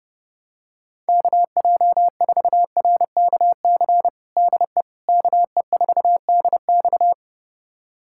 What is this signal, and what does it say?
Morse code, heard as a keyed tone, about 30 wpm, 700 Hz. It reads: KJ4RKC DE KE4DX
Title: USA Callsign DE Callsign 30wpm